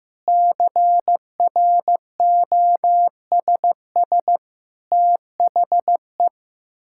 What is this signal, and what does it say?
Morse code, heard as a keyed tone, about 15 wpm, 700 Hz. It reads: CROSS THE
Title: Sentences from Top 1000 Words - Review Part 6 - 15wpm